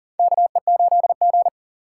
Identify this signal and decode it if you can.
Morse code, heard as a keyed tone, about 40 wpm, 700 Hz. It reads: KE8G
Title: USA Call Signs - 4x Character Spacing - 40wpm